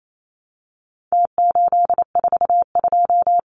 Morse code, heard as a keyed tone, about 28 wpm, 700 Hz. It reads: T842
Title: Numbers - Contest Style - 4-Digit 28wpm